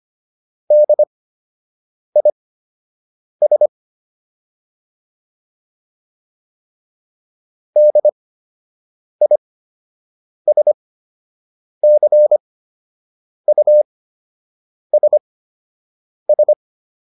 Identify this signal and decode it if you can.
Morse code, heard as a keyed tone, about 25 wpm, spaced as for 7 wpm, 600 Hz. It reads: DIS DISCUSS
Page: fundamental-practice-files